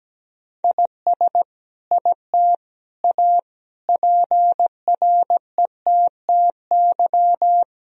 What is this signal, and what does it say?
Morse code, heard as a keyed tone, about 17 wpm, 700 Hz. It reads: IS IT A PRETTY
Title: Sentences from Top 1000 Words - Review Part 2 - 17wpm